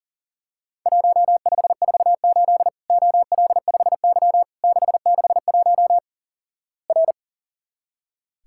Morse code, heard as a keyed tone, about 44 wpm, 700 Hz. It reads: 1548 OL5Y 661 R
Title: Worked All Europe DX Contest - QTC Exchange - 40wpm